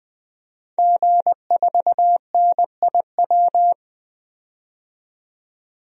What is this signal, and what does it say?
Morse code, heard as a keyed tone, about 21 wpm, 700 Hz. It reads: G4NIW
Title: DXCC-100 Call Signs - 5 Characters - 20wpm